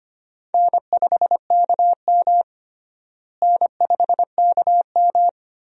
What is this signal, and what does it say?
Morse code, heard as a keyed tone, about 25 wpm, 700 Hz. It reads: N5KM N5KM
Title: USA Call Signs - 4 Characters - Repeated 3x - 25wpm